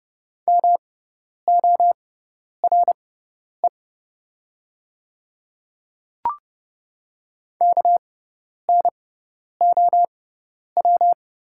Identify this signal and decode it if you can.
Morse code, heard as a keyed tone, about 30 wpm, spaced as for 10 wpm, 690 Hz. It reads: MORE KNOW
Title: 6x Character Spacing - Top 200 Words - 30wpm